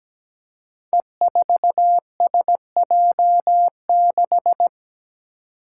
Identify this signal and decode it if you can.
Morse code, heard as a keyed tone, about 17 wpm, 700 Hz. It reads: E4SJ6
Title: Five Challenging Letter-Number Combinations 17wpm